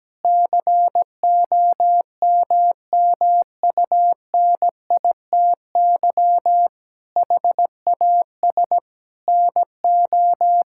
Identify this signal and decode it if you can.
Morse code, heard as a keyed tone, about 17 wpm, 700 Hz. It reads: COMMUNITY HAS NO
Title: Sentences from Top 1000 Words - Review Part 5 - 17wpm